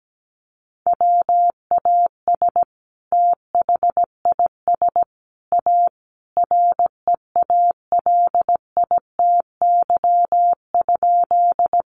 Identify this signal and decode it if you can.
Morse code, heard as a keyed tone, about 17 wpm, 700 Hz. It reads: WAS THIS A REALITY?
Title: Sentences from Top 1200 Words 17wpm